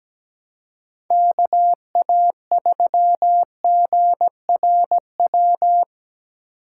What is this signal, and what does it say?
Morse code, heard as a keyed tone, about 19 wpm, 700 Hz. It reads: KA3GRW
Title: Speed Racing - USA 2x3 Call Signs 17wpm